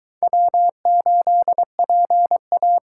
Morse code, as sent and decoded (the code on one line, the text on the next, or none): .-- ---.. .--. .-
W8PA